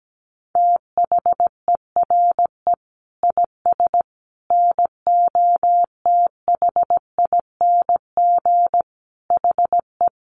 Morse code, heard as a keyed tone, about 17 wpm, 700 Hz. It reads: THERE IS NOTHING HE
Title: Sentences from Top 600 Words 17wpm